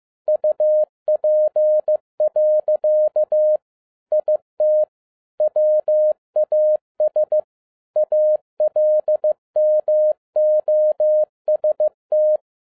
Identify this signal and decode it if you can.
Morse code, heard as a keyed tone, about 15 wpm, 610 Hz. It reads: UP. IT WAS ALMOST